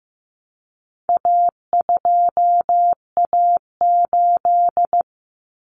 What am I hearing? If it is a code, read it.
Morse code, heard as a keyed tone, about 15 wpm, 700 Hz. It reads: A2A8
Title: Common Cut Numbers - 4-Digit 15wpm